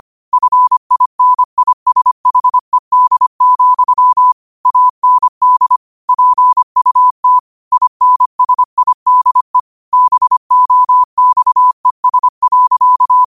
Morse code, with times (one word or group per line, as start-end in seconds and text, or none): 0.3-4.3 FINISHED,
4.7-5.8 AND
6.1-7.4 PUT
7.7-9.6 INSIDE
9.9-13.2 BOXES.